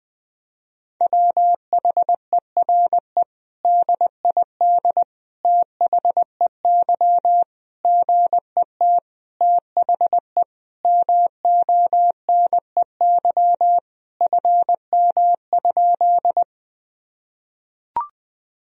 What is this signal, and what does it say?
Morse code, heard as a keyed tone, about 20 wpm, 700 Hz. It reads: WHERE DID THEY GET THE MONEY FM?
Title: Sentences from Top 400 Words and Abbreviations 20wpm